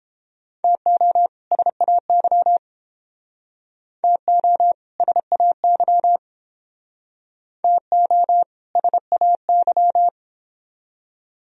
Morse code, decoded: TO SAY TO SAY TO SAY

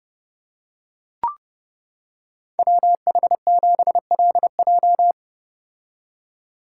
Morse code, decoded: WH7LJ

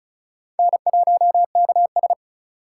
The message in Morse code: N1KS

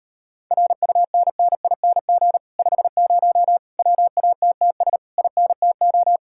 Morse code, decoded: RUNNING 50 WATTS INTO

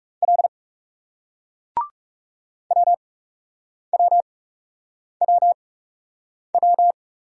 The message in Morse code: L W W W W

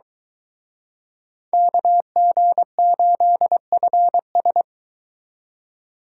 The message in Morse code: KG8FS

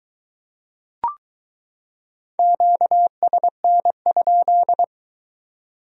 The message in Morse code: QSN?

